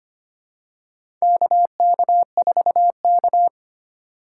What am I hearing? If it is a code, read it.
Morse code, heard as a keyed tone, about 25 wpm, 700 Hz. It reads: KK4K
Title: USA Call Signs - 4x Character Spacing - 25wpm